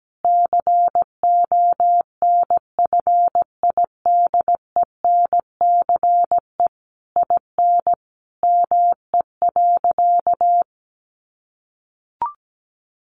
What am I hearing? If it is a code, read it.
Morse code, heard as a keyed tone, about 17 wpm, 700 Hz. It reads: CONFIDENCE IN ME.
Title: Sentences from Top 1500 Words 17wpm